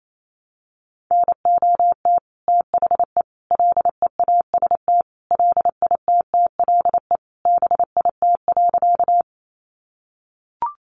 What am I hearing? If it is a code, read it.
Morse code, heard as a keyed tone, about 28 wpm, 700 Hz. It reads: NOT THE LEAST LITTLE BIT.
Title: Sentences from Top 900 Words 28wpm